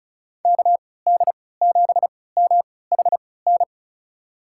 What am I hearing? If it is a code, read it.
Morse code, heard as a keyed tone, about 34 wpm, spaced as for 20 wpm, 700 Hz. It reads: KD7MHN
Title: USA Call Signs - 6 Characters - 3x Character Spacing - 35wpm